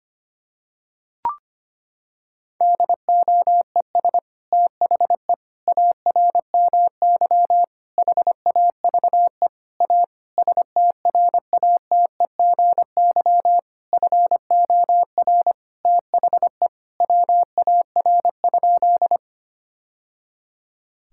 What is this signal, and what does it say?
Morse code, heard as a keyed tone, about 25 wpm, 700 Hz. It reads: DOES THE ARMY HAVE A STRATEGY FOR THE WAR?
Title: Sentences from Top 1000 Words - Review Part 5 - 25wpm